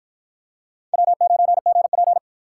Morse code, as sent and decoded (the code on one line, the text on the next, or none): .-- ----. --. .--.
W9GP